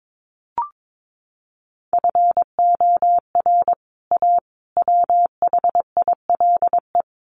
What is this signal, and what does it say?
Morse code, heard as a keyed tone, about 22 wpm, 700 Hz. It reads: FOR A WHILE
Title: Sets of 3 Words 22wpm